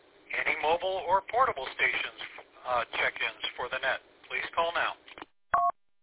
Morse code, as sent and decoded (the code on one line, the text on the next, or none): none